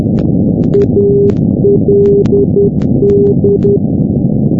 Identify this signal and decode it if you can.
Morse code, heard as a keyed tone, about 10 wpm, 400 Hz, with noise. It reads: ALD
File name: europe / g-910-ald-383-sdr-hu-g-south west uk- -1245z